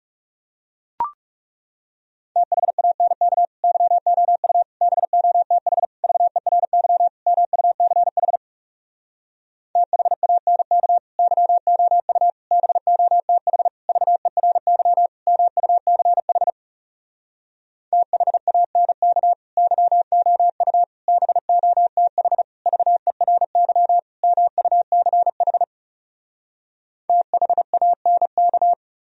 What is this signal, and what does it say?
Morse code, heard as a keyed tone, about 40 wpm, 700 Hz. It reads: THANK YOU BOTH VERY MUCH THANK YOU BOTH VERY MUCH THANK YOU BOTH VERY MUCH THANK